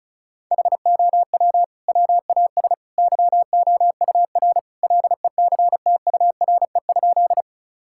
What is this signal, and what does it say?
Morse code, heard as a keyed tone, about 36 wpm, 700 Hz. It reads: HOW WAS YOUR LECTURE?
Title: Sentences from Top 1600 Words 35wpm